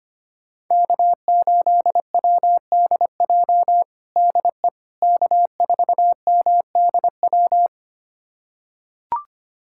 Morse code, heard as a keyed tone, about 25 wpm, 700 Hz. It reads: K8WDJ DE K4MDW
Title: USA Callsign DE Callsign - 5 Characters - 25wpm